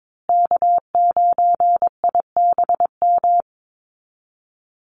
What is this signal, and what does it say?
Morse code, heard as a keyed tone, about 22 wpm, 700 Hz. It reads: K9IBM
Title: Call Signs Repeated 2x - 22wpm